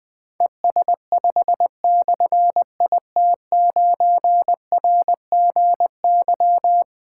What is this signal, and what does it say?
Morse code, heard as a keyed tone, about 20 wpm, 700 Hz. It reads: ES5/IT9RGY
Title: Difficult Call Signs 20wpm